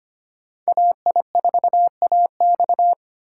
Morse code, heard as a keyed tone, about 32 wpm, 700 Hz. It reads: AI4AX